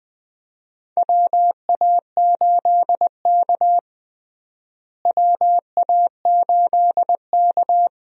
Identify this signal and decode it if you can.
Morse code, heard as a keyed tone, about 21 wpm, 700 Hz. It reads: WA8K WA8K
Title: USA Call Signs - 4 Characters - Repeated 2x - 20wpm